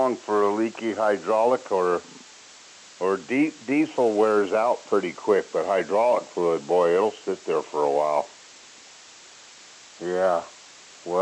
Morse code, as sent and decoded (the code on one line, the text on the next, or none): none